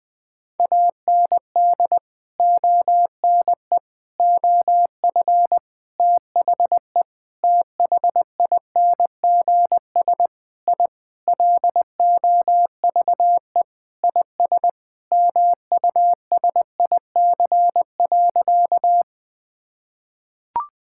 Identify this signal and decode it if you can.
Morse code, heard as a keyed tone, about 20 wpm, 700 Hz. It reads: AND ONE OF THE THINGS I LOVE IS MUSIC.
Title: Sentences with 5-Word Phrases - 20wpm